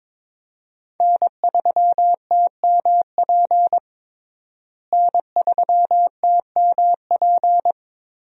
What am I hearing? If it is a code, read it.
Morse code, heard as a keyed tone, about 22 wpm, 700 Hz. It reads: N3TMP N3TMP
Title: USA Call Signs - 5 Characters - Repeated 2x - 22wpm